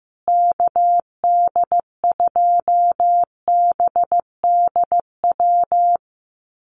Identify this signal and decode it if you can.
Morse code, heard as a keyed tone, about 15 wpm, 700 Hz. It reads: KD2BDW
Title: USA Call Signs Repeated 3x 15wpm